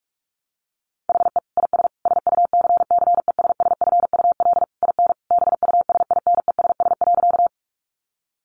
Morse code, heard as a keyed tone, about 44 wpm, 700 Hz. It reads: HE IS SUCCESSFUL IN BUSINESS.
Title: Sentences from Top 900 Words 45wpm